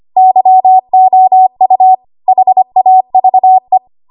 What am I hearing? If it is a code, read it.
Morse code, heard as a keyed tone, about 25 wpm, 750 Hz. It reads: YOU HAVE
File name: Archive / 25 WPM / 25WPM